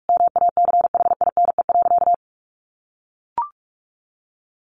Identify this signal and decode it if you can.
Morse code, heard as a keyed tone, about 44 wpm, 700 Hz. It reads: MACHINE.
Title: A Sound Of Thunder 45wpm